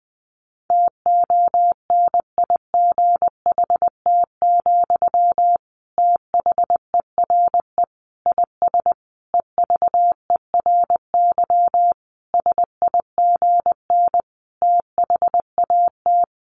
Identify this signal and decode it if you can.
Morse code, heard as a keyed tone, about 20 wpm, 700 Hz. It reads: TONIGHT, THERE IS EVERY SIGN THAT
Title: Sentences from Top 1400 Words 20wpm